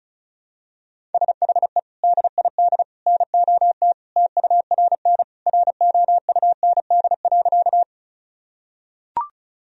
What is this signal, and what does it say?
Morse code, heard as a keyed tone, about 34 wpm, 700 Hz. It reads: SHE DID NOT TURN ROUND.
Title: Sentences from Top 1000 Words - Review Part 1 - 35wpm